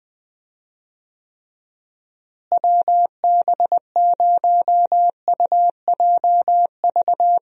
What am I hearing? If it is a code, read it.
Morse code, heard as a keyed tone, about 20 wpm, 700 Hz. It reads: WB0UJV
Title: USA Call Signs - 6 Characters - Repeated 2x - 20wpm